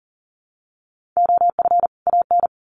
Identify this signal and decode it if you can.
Morse code, heard as a keyed tone, about 40 wpm, 700 Hz. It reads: OF AN
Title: Sets of 2 Words 40wpm